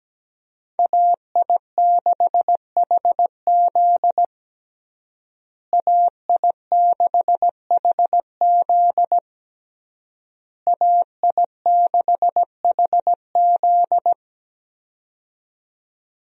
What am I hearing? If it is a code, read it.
Morse code, heard as a keyed tone, about 17 wpm, 700 Hz. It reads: AI6HZ AI6HZ AI6HZ
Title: USA Call Signs - 5 Characters - Repeated 3x - 17wpm